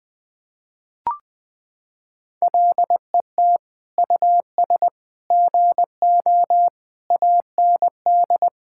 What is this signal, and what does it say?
Morse code, heard as a keyed tone, about 21 wpm, 700 Hz. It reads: LET US GO AND
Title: Sentences from Top 1600 Words 20wpm